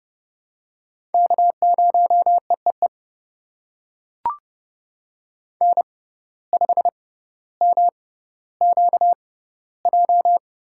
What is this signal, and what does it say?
Morse code, heard as a keyed tone, about 32 wpm, spaced as for 16 wpm, 700 Hz. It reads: K0EEE N5MQJ
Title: USA Call Signs - 5 Characters - 6x Character Spacing - 30wpm